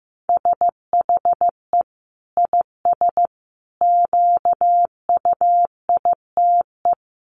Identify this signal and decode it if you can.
Morse code, heard as a keyed tone, about 15 wpm, 700 Hz. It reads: SHE IS QUITE
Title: Sentences from Top 1500 Words 15wpm